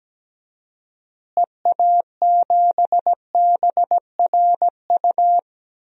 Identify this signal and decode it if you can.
Morse code, heard as a keyed tone, about 17 wpm, 700 Hz. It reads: EA7BRU
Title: DXCC-100 Call Signs - 6 Characters - Repeated 3x - 17wpm